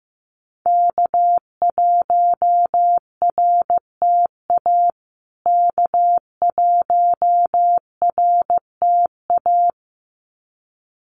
Morse code, heard as a keyed tone, about 15 wpm, 700 Hz. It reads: K1RTA K1RTA
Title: USA Call Signs Repeated 2x 15wpm